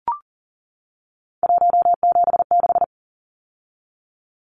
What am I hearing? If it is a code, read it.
Morse code, heard as a keyed tone, about 40 wpm, 700 Hz. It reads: 176
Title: Numbers - 3-Digit 40wpm